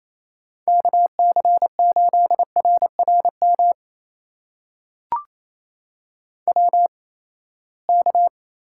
Morse code, heard as a keyed tone, about 28 wpm, spaced as for 19 wpm, 700 Hz. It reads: KC8RRM WK
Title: USA Call Signs - 8x Character Spacing - 28wpm